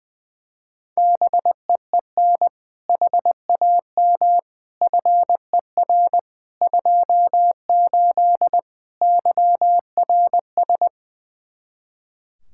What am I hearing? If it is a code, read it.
Morse code, heard as a keyed tone, about 20 wpm, 700 Hz. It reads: BEEN HAM FER 28 YRS
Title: K1USN SST QSO Practice 20wpm